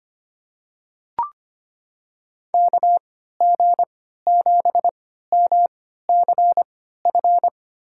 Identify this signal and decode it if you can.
Morse code, heard as a keyed tone, about 25 wpm, spaced as for 14 wpm, 700 Hz. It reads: KG7MCF
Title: USA Call Signs - 6 Characters - 3x Character Spacing - 25wpm